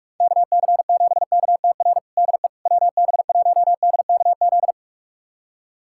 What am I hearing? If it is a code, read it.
Morse code, heard as a keyed tone, about 41 wpm, 700 Hz. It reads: KC7KTR DE WB1DKZ
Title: USA Callsign DE Callsign 45wpm